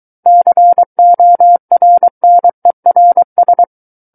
Morse code, heard as a keyed tone, about 23 wpm, 700 Hz. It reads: CORNERS
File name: Book One / 14-Chapter 14 - 23 WPM